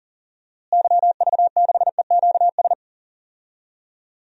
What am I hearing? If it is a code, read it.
Morse code, heard as a keyed tone, about 40 wpm, 700 Hz. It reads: YV6EQS
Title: DXCC-100 Call Signs - 6 Characters - Repeated 3x - 40wpm